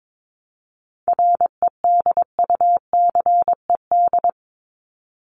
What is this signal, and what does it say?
Morse code, heard as a keyed tone, about 22 wpm, 700 Hz. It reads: REDUCED
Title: ED Suffix - Top 100 Words with Base Words 22wpm